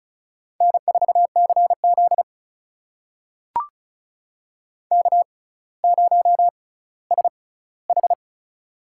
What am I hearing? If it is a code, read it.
Morse code, heard as a keyed tone, about 36 wpm, spaced as for 18 wpm, 700 Hz. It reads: N4CZ K0SH
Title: USA Call Signs - 4 Characters - 6x Character Spacing - 35wpm